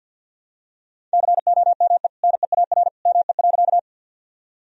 Morse code, heard as a keyed tone, about 49 wpm, 700 Hz. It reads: COME NEAR ME.
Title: Sentences from Top 1000 Words - Review Part 1 - 50wpm